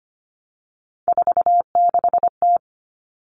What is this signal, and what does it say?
Morse code, heard as a keyed tone, about 25 wpm, 700 Hz. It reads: 46T
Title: All Cut Numbers - 3-Digit 25wpm